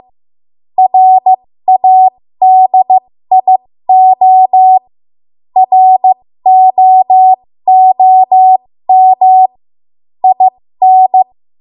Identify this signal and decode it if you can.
Morse code, heard as a keyed tone, about 15 wpm, spaced as for 13 wpm, 750 Hz. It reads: RADIO ROOM IN
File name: Morse / 13WPM